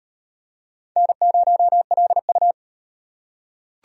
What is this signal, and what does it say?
Morse code, heard as a keyed tone, about 37 wpm, 700 Hz. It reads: N0LU